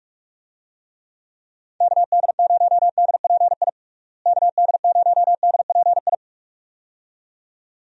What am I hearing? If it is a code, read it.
Morse code, heard as a keyed tone, about 46 wpm, 700 Hz. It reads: KD0DPI KD0DPI
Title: USA Call Signs - 6 Characters - Repeated 2x - 45wpm